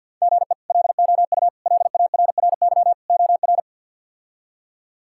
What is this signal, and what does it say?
Morse code, heard as a keyed tone, about 51 wpm, 700 Hz. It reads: GE LOU LARRY OR
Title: K1USN SST QSO Practice - 50wpm